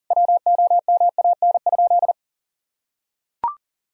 Morse code, heard as a keyed tone, about 40 wpm, 700 Hz. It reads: WOMAN?
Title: Sentences from Top 1000 Words - Review Part 3 - 40wpm